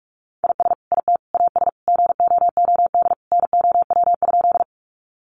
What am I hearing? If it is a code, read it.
Morse code, heard as a keyed tone, about 44 wpm, 700 Hz. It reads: IS IT AS GOOD NOW?
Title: Sentences from Top 200 Words 45wpm